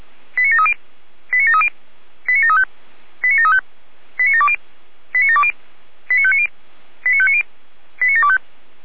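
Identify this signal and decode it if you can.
Five-tone selective call (ZVEI), heard as quick runs of five tones, tones of about 70 ms. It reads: 89730 89730 89735 89735 89720 89720 89590 89590 89725
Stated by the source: ZVEI five-tone selective call received by SDR.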